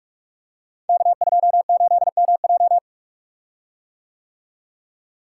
Morse code, heard as a keyed tone, about 47 wpm, 700 Hz. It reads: K28MJ
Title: Code Groups - 5 Letter:Numbers - 45wpm